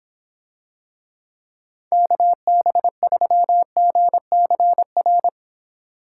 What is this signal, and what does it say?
Morse code, heard as a keyed tone, about 24 wpm, 700 Hz. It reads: KB3GCR